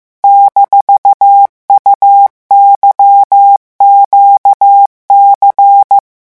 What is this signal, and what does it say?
Morse code, heard as a keyed tone, about 15 wpm, 800 Hz. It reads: -UYQC
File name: morse mp3 / revision17-vitesse 14 mots